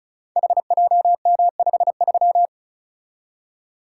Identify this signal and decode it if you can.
Morse code, heard as a keyed tone, about 36 wpm, 700 Hz. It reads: HJM53
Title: Code Groups - 5 Letter:Numbers - 35wpm